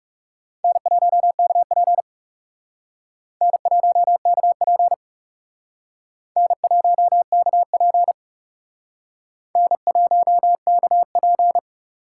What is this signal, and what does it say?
Morse code, heard as a keyed tone, about 36 wpm, 700 Hz. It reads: N1KP N1KP N1KP N1KP
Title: Speed Racing - USA 1x2 Call Signs 30wpm